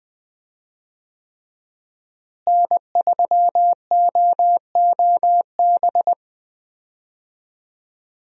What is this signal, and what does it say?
Morse code, heard as a keyed tone, about 21 wpm, 700 Hz. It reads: N3OOB
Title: USA Call Signs - 5 Characters - Repeated 3x - 20wpm